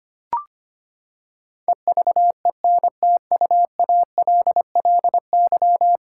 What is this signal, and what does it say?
Morse code, heard as a keyed tone, about 25 wpm, 700 Hz. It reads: EVENTUALLY
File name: Ten Letter Words - 25wpm